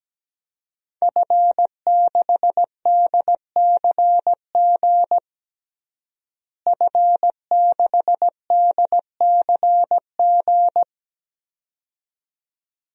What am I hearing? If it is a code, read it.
Morse code, heard as a keyed tone, about 17 wpm, 700 Hz. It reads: F6DCG F6DCG
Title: DXCC-100 Call Signs - 5 Characters - Repeated 2x - 17wpm